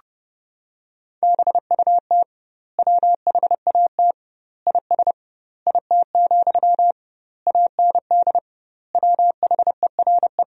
Morse code, heard as a keyed tone, about 30 wpm, 700 Hz. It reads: BUT WHAT IS IT, AND WHERE
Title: 2x Word-Spacing - Sentences from Top 300 Words - 30wpm